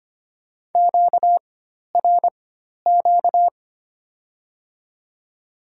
Morse code, heard as a keyed tone, about 25 wpm, spaced as for 12 wpm, 700 Hz. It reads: QRQ
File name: Top 90 QSO Elements - 4x Character Spacing - 25wpm